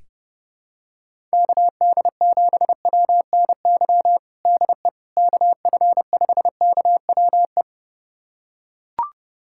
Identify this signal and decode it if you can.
Morse code, heard as a keyed tone, about 30 wpm, 700 Hz. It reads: KD7WNY DE KF5KWE
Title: USA Callsign DE Callsign - 6 Characters - 30wpm